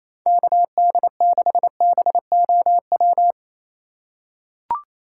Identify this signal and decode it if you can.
Morse code, heard as a keyed tone, about 28 wpm, 700 Hz. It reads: KD6BOW
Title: USA Call Signs - 2x Character Spacing - 28wpm